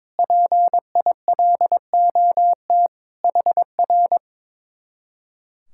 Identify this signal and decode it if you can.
Morse code, heard as a keyed tone, about 22 wpm, 700 Hz. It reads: PILOT HR
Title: Speed-Racing Common QSO Phrases - 22wpm